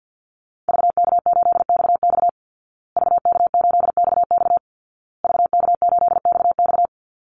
Morse code, heard as a keyed tone, about 50 wpm, 700 Hz. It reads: VK8XX VK8XX VK8XX
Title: Call Signs Repeated 2x - 50wpm